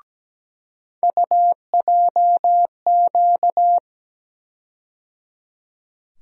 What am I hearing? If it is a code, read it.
Morse code, heard as a keyed tone, about 17 wpm, 700 Hz. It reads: UJQ
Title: USA Call Sign Postfixes - 3 Letters 17wpm